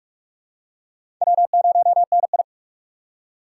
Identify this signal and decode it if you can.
Morse code, heard as a keyed tone, about 46 wpm, 700 Hz. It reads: W0NI
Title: USA Call Signs - 4 Characters - 2x Character Spacing - 45wpm